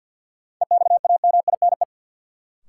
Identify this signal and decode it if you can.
Morse code, heard as a keyed tone, about 53 wpm, 700 Hz. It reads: EXAMINE
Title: Sentences from Top 1600 Words - New Word Drill - 50wpm